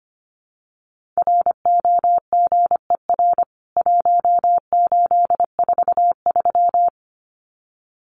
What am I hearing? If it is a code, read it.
Morse code, heard as a keyed tone, about 25 wpm, 700 Hz. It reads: ROGER 1843
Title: CWT Exchange 25wpm